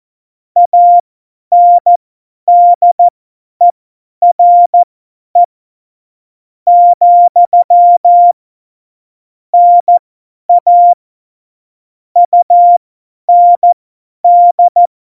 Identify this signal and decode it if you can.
Morse code, heard as a keyed tone, about 14 wpm, spaced as for 9 wpm, 700 Hz. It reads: ANDERE, NA UND
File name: Morsekurs / Witze / Witz 09. Oktober 2018, 50 F70